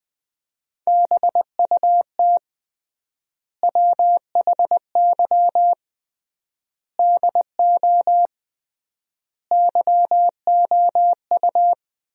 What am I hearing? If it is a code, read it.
Morse code, heard as a keyed tone, about 20 wpm, 700 Hz. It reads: BUT WHY DO YOU
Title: 3x Word-Spacing - Sentences from Top 200 Words - 20wpm